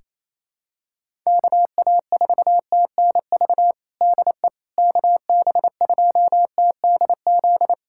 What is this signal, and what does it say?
Morse code, heard as a keyed tone, about 28 wpm, 700 Hz. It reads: KA4TNV DE KB2TDZ
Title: USA Callsign DE Callsign - 6 Characters - 28wpm